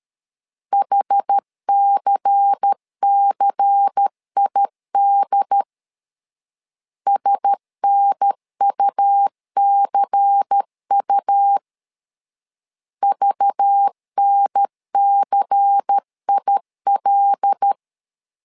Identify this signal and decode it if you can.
Morse code, heard as a keyed tone, about 13 wpm, 800 Hz. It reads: HCCID SNUCU VNCIL
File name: LEZIONI / 5 LBVC / 5 LBVC 12wpm 1955